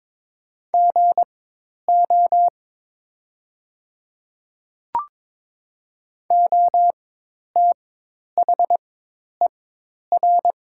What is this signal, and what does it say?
Morse code, heard as a keyed tone, about 22 wpm, spaced as for 10 wpm, 700 Hz. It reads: GO OTHER
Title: 4x Character Spacing - Top 100 Words - 22wpm